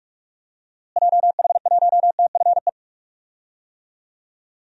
Morse code, heard as a keyed tone, about 46 wpm, 700 Hz. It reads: JH1TFE
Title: DXCC-100 Call Signs - 6 Characters - Repeated 2x - 45wpm